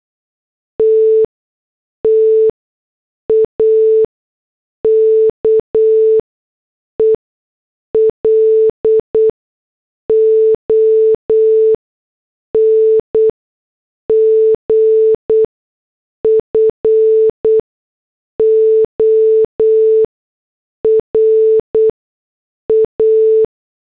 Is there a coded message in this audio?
Yes: Morse code from 0.8 to 23.4 s, heard as a keyed tone, about 8 wpm, spaced as for 6 wpm, 440 Hz. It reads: T TAKE LONG FOR A